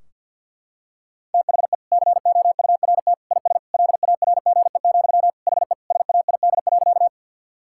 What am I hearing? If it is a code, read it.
Morse code, heard as a keyed tone, about 52 wpm, 700 Hz. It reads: THE COURT IS LARGE, HE SAID.
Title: Sentences from Top 1600 Words 50wpm